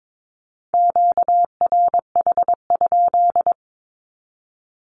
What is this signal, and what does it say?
Morse code, heard as a keyed tone, about 22 wpm, 700 Hz. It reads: QRH?
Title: Q-Codes 22wpm